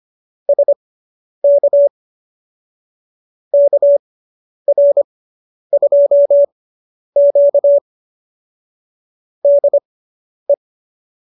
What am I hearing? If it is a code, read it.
Morse code, heard as a keyed tone, about 25 wpm, spaced as for 10 wpm, 570 Hz. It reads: SK KR2Q DE